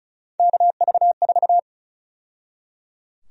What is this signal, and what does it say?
Morse code, heard as a keyed tone, about 34 wpm, 700 Hz. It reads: KV4
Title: USA Call Sign Prefixes - 35wpm